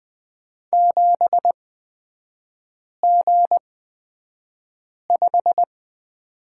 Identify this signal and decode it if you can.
Morse code, heard as a keyed tone, about 20 wpm, spaced as for 5 wpm, 700 Hz. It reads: 7G5
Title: 255 Lesson 41e - Period - Three Characters 20wpm